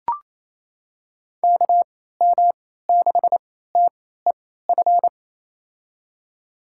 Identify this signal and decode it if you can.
Morse code, heard as a keyed tone, about 28 wpm, spaced as for 16 wpm, 700 Hz. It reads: KM6TEF
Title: USA Call Signs - 3x Character Spacing - 28wpm